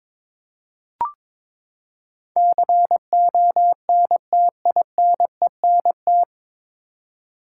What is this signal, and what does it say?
Morse code, heard as a keyed tone, about 22 wpm, 700 Hz. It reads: CONTINENT
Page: Morse Code Ninja practice 22wpm